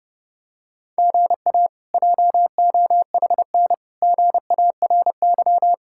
Morse code, heard as a keyed tone, about 30 wpm, 700 Hz. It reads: GA JOHN GARY
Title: K1USN SST QSO Practice - 30wpm